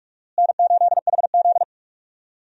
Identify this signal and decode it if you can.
Morse code, heard as a keyed tone, about 47 wpm, 700 Hz. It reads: N8HZ
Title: USA Call Signs - 4 Characters - Repeated 3x - 45wpm